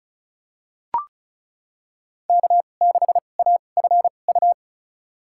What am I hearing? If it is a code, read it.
Morse code, heard as a keyed tone, about 34 wpm, spaced as for 25 wpm, 700 Hz. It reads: K6AFU
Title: USA Call Signs - 5 Characters - 2x Character Spacing - 35wpm